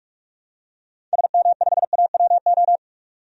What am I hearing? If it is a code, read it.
Morse code, heard as a keyed tone, about 44 wpm, 700 Hz. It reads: SM5AWO
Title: DXCC-100 Call Signs - 6 Characters - 45wpm